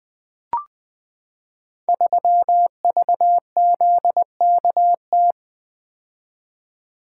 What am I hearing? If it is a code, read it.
Morse code, heard as a keyed tone, about 20 wpm, 700 Hz. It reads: 3VZKT
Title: Code Groups - 5 Letter:Numbers - 20wpm